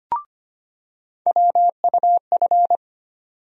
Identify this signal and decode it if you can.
Morse code, heard as a keyed tone, about 25 wpm, 700 Hz. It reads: WUF
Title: USA Call Sign Postfixes - 3 Letters 25wpm